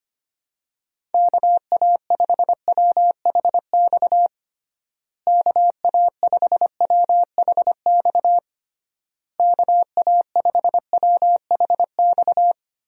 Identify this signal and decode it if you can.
Morse code, heard as a keyed tone, about 25 wpm, 700 Hz. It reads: KA5WHX KA5WHX KA5WHX
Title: USA Call Signs - 6 Characters - Repeated 3x - 25wpm